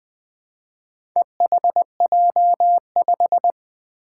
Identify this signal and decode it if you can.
Morse code, heard as a keyed tone, about 20 wpm, 700 Hz. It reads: EHJ5
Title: Four Challenging Letter-Number Combinations 20wpm